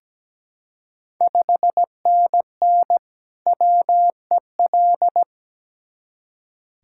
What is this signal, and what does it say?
Morse code, heard as a keyed tone, about 17 wpm, 700 Hz. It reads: 5NN WEL
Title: Ontario QSO Party Exchange 17wpm